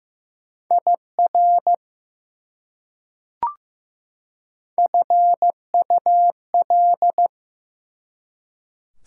Morse code, heard as a keyed tone, about 15 wpm, 700 Hz. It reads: IR FUL
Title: Prefixes and Suffixes 15wpm